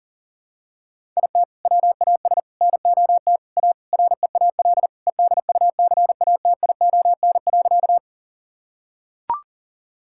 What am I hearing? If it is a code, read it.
Morse code, heard as a keyed tone, about 37 wpm, 700 Hz. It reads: IT WAS NOT A REAL EDUCATION.
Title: Sentences from Top 1000 Words - Review Part 6 - 40wpm